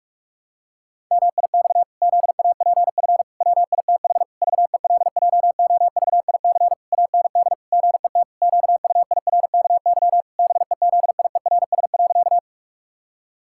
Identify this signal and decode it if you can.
Morse code, heard as a keyed tone, about 46 wpm, 700 Hz. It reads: MIX ZAPF WITH VELJOVIC AND GET QUIRKY BEZIERS.
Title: Pangram Challenge v2 - 45wpm